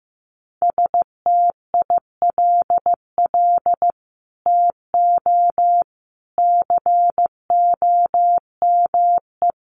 Morse code, heard as a keyed tone, about 15 wpm, 700 Hz. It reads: STILL TO COME
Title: Sets of 5 Words Encore2 15wpm